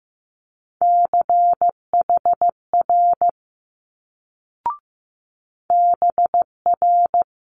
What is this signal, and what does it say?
Morse code, heard as a keyed tone, about 15 wpm, 700 Hz. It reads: CHR BR
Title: Multi-letter Phonemes Quick Fire 15wpm